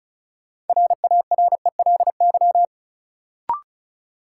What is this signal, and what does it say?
Morse code, heard as a keyed tone, about 36 wpm, 700 Hz. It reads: RARELY
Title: Sentences from Top 1400 Words - New Word Drill Alt - 35wpm